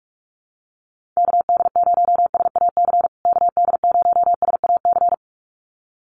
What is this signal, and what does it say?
Morse code, heard as a keyed tone, about 44 wpm, 700 Hz. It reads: KD0SAC KD0SAC
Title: USA Call Signs Repeated 2x 45wpm